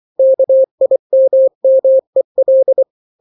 Morse code, heard as a keyed tone, about 24 wpm, 530 Hz. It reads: KIMMEL
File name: daily / 23wpm